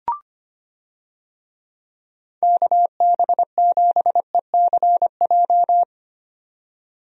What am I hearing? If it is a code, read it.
Morse code, heard as a keyed tone, about 25 wpm, 700 Hz. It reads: KB7ECJ
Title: USA Call Signs - 6 Characters - 25wpm